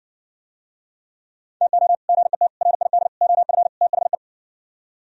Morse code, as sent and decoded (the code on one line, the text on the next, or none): - --- / --. - / .-. .. -.. / --- ..-. / - .... .
TO GET RID OF THE